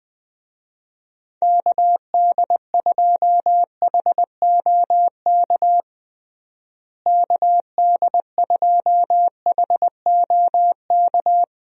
Morse code, heard as a keyed tone, about 20 wpm, 700 Hz. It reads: KD2HOK KD2HOK
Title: USA Call Signs - 6 Characters - Repeated 2x - 20wpm